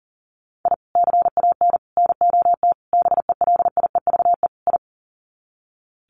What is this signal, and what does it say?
Morse code, heard as a keyed tone, about 40 wpm, 700 Hz. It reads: I CAN NOT BELIEVE I
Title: Sets of 5 Words Encore2 40wpm